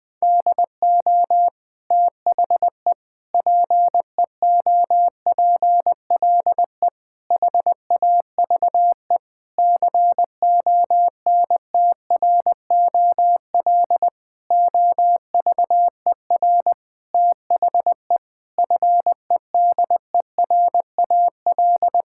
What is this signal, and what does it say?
Morse code, heard as a keyed tone, about 21 wpm, 700 Hz. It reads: DO THE PEOPLE HAVE CONTROL OVER THE FEDERAL
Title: Sentences from Top 1000 Words - Review Part 5 - 20wpm